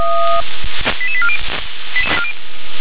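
Five-tone selective call (ZVEI), heard as quick runs of five tones, tones of about 70 ms. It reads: 88840
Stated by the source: ZVEI five-tone selective call received on SDR.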